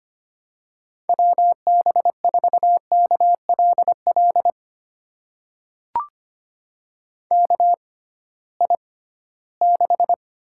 Morse code, heard as a keyed tone, about 25 wpm, spaced as for 16 wpm, 700 Hz. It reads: WB4KLL KI6
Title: USA Call Signs - 6 Characters - 6x Character Spacing - 25wpm